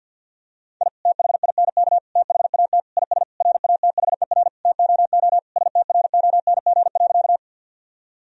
Morse code, heard as a keyed tone, about 50 wpm, 700 Hz. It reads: I THINK THAT IS RATHER TOO STRONG.
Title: Sentences from Top 1000 Words - Review Part 4 - 50wpm